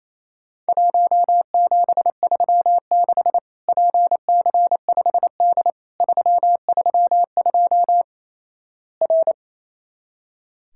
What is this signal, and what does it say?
Morse code, heard as a keyed tone, about 28 wpm, 700 Hz. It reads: 1736 PC5D 332 R
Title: Worked All Europe DX Contest - QTC Exchange - 28wpm